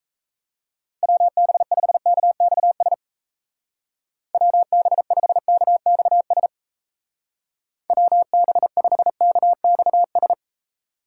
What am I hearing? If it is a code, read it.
Morse code, heard as a keyed tone, about 34 wpm, 700 Hz. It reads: WB5KXS WB5KXS WB5KXS
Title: Speed Racing - USA Call Signs 28wpm